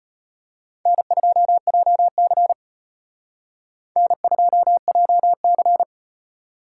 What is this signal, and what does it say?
Morse code, heard as a keyed tone, about 38 wpm, 700 Hz. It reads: N2JC N2JC